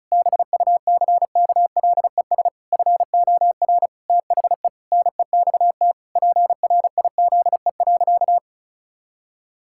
Morse code, heard as a keyed tone, about 36 wpm, 700 Hz. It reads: BUCKLES FOR THE NEXT PRIZE.
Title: Pangram Challenge v2 - 35wpm